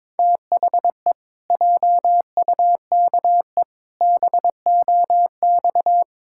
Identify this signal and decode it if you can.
Morse code, heard as a keyed tone, about 22 wpm, 700 Hz. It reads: THE JUKE BOX
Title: Pangram Challenge v2 - 22wpm